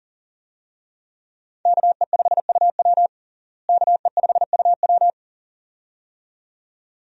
Morse code, heard as a keyed tone, about 40 wpm, 700 Hz. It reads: KE5UW KE5UW
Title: USA Call Signs - 5 Characters - Repeated 2x - 40wpm